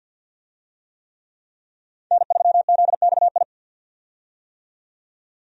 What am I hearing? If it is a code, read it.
Morse code, heard as a keyed tone, about 51 wpm, 700 Hz. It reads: N3ZXI
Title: USA Call Signs - 5 Characters - 50wpm